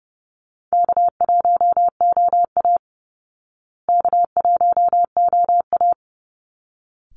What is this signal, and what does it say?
Morse code, heard as a keyed tone, about 30 wpm, 700 Hz. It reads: K1OA K1OA
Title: Call Signs Repeated 3x - 30wpm